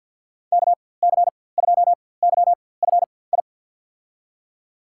Morse code, heard as a keyed tone, about 48 wpm, spaced as for 23 wpm, 700 Hz. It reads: KC2YFI
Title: USA Call Signs - 6 Characters - 4x Character Spacing - 50wpm